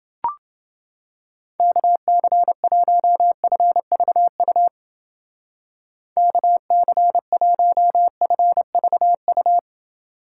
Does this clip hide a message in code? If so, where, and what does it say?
Yes: Morse code from 1.6 to 9.6 s, heard as a keyed tone, about 29 wpm, 700 Hz. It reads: KC1FVU KC1FVU